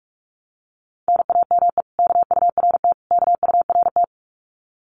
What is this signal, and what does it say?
Morse code, heard as a keyed tone, about 46 wpm, 700 Hz. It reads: NAME KURT KURT
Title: Common QSO Phrases Encore 45wpm